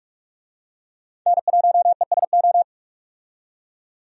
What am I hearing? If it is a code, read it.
Morse code, heard as a keyed tone, about 44 wpm, 700 Hz. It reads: N1ESO
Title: USA Call Signs - 5 Characters - 45wpm